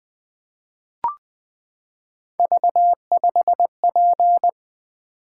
Morse code, heard as a keyed tone, about 20 wpm, 700 Hz. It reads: V5P
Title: Three Letter-Number Combinations - 20wpm